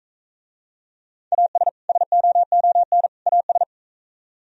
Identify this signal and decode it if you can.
Morse code, heard as a keyed tone, about 44 wpm, 700 Hz. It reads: AS SOON AS